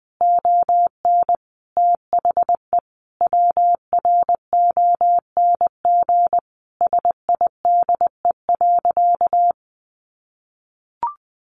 Morse code, heard as a keyed tone, about 20 wpm, 700 Hz. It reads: ON THE WRONG SIDE.
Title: Sentences from Top 900 Words 20wpm